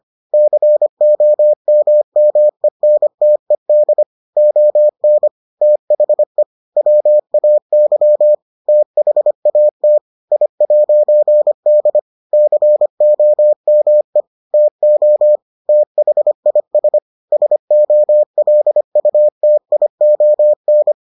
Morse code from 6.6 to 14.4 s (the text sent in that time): WAY THAT I'D COME